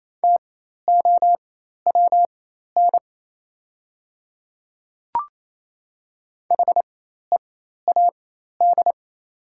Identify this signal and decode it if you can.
Morse code, heard as a keyed tone, about 28 wpm, spaced as for 13 wpm, 700 Hz. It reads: TOWN HEAD
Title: 4x Character Spacing - Top 500 Words - 28wpm